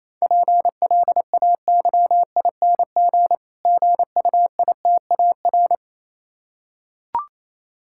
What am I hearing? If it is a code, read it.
Morse code, heard as a keyed tone, about 29 wpm, 700 Hz. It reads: PLAYING GUITAR